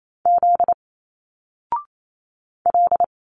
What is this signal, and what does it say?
Morse code, heard as a keyed tone, about 28 wpm, 700 Hz. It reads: Z L
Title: Single Letters 28wpm